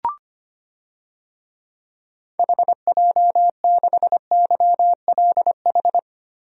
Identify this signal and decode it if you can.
Morse code, heard as a keyed tone, about 25 wpm, 700 Hz. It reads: HJ6YLH
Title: DXCC-100 Call Signs - 6 Characters - 25wpm